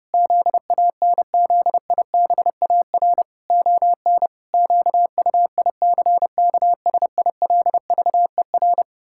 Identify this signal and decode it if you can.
Morse code, heard as a keyed tone, about 30 wpm, 700 Hz. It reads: ZANZIBAR ON QUICKSILVER
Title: Pangram Challenge v2 - 30wpm